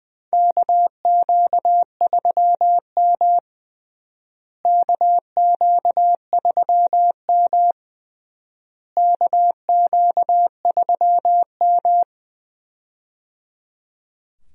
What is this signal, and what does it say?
Morse code, heard as a keyed tone, about 20 wpm, 700 Hz. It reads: KQ3M KQ3M KQ3M
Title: USA Call Signs - 4 Characters - Repeated 3x - 20wpm